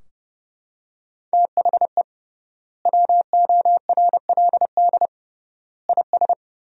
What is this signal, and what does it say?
Morse code, heard as a keyed tone, about 32 wpm, 700 Hz. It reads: THE WORLD IS
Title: 3x Word-Spacing - Sentences from Top 400 Words - 30wpm